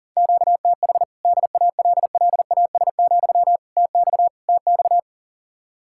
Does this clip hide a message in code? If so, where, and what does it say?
Yes: Morse code from 0.2 to 5.0 s, heard as a keyed tone, about 51 wpm, 700 Hz. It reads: QTH DALLAS, TX TX